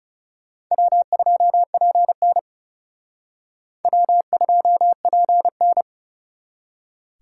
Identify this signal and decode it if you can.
Morse code, heard as a keyed tone, about 37 wpm, 700 Hz. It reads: W2PN W2PN
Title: Speed Racing - USA 1x2 Call Signs 30wpm